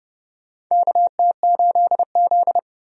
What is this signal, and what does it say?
Morse code, heard as a keyed tone, about 32 wpm, 700 Hz. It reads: KT8Z
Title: USA Call Signs - 4 Characters - 30wpm